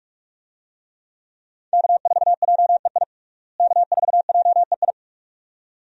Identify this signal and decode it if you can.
Morse code, heard as a keyed tone, about 42 wpm, 700 Hz. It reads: K4JEI K4JEI
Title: USA Call Signs - 5 Characters - Repeated 2x - 45wpm